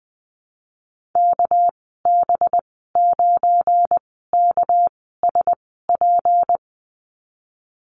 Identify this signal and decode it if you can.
Morse code, heard as a keyed tone, about 19 wpm, spaced as for 14 wpm, 700 Hz. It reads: KB9KSP
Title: 238 Lesson 38g - Letter q - Call Signs 20wpm